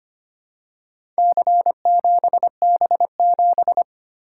Morse code, heard as a keyed tone, about 25 wpm, 700 Hz. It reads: C7B7
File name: practice / 25wpm / Four Letter-Number Combinations - 25wpm